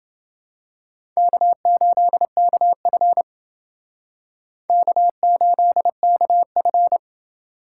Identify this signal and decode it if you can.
Morse code, heard as a keyed tone, about 28 wpm, 700 Hz. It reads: K8KF K8KF